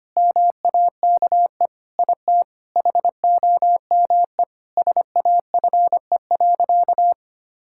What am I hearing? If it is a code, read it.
Morse code, heard as a keyed tone, about 25 wpm, 700 Hz. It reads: MAKE IT HOME SAFE.
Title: Sentences from Top 1600 Words 25wpm